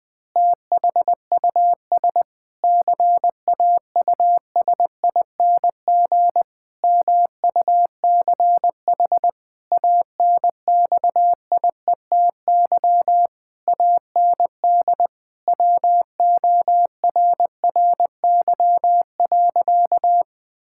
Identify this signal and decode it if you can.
Morse code, heard as a keyed tone, about 20 wpm, 700 Hz. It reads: THUS CAUSING MUCH ANXIETY AND WORRY.
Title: Pangram Challenge v2 - 20wpm